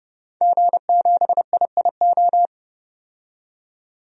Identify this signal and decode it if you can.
Morse code, heard as a keyed tone, about 30 wpm, 700 Hz. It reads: G7IIO
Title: DXCC-100 Call Signs - 5 Characters - 30wpm